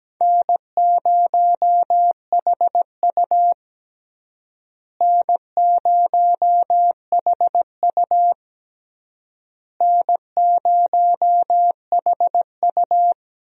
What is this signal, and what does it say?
Morse code, heard as a keyed tone, about 17 wpm, 700 Hz. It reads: N0HU N0HU N0HU
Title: USA Call Signs - 4 Characters - Repeated 3x - 17wpm